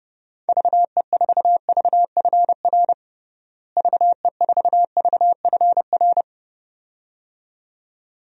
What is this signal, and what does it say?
Morse code, heard as a keyed tone, about 30 wpm, 700 Hz. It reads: VE4VFR VE4VFR
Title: DXCC-100 Call Signs - 6 Characters - Repeated 2x - 30wpm